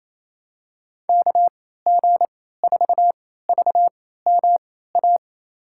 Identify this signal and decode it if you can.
Morse code, heard as a keyed tone, about 28 wpm, spaced as for 16 wpm, 700 Hz. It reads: KG4VMA
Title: USA Call Signs - 6 Characters - 3x Character Spacing - 28wpm